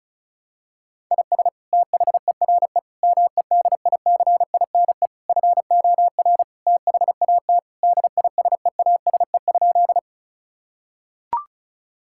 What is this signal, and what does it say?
Morse code, heard as a keyed tone, about 36 wpm, 700 Hz. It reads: IS THERE MEDICINE FOR THAT DISEASE?
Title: Sentences from Top 1000 Words - Review Part 6 - 35wpm